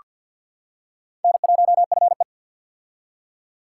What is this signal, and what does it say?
Morse code, heard as a keyed tone, about 47 wpm, 700 Hz. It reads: N1FE
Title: K1USN SST QSO Practice - 50wpm